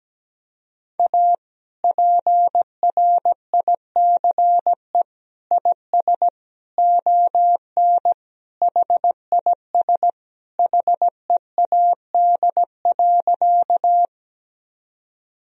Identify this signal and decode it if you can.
Morse code, heard as a keyed tone, about 17 wpm, 700 Hz. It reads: A PRICE IS ON HIS HEAD.
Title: Sentences from Top 1000 Words - Review Part 4 - 17wpm